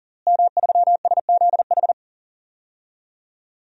Morse code, heard as a keyed tone, about 40 wpm, 700 Hz. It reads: M3SZH
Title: DXCC-100 Call Signs - 5 Characters - Repeated 3x - 40wpm